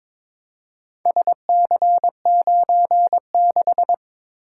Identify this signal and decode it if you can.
Morse code, heard as a keyed tone, about 23 wpm, 700 Hz. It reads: SC96
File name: practice / 22wpm / Four Letter-Number Combinations - 22wpm